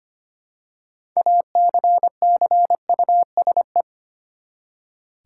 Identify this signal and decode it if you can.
Morse code, heard as a keyed tone, about 25 wpm, 700 Hz. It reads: ACCUSE
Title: Six Letter Words - 25wpm